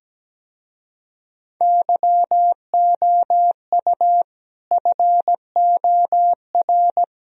Morse code, heard as a keyed tone, about 17 wpm, 700 Hz. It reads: YOU FOR
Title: Speed-Racing - Sets of 5 Words 17wpm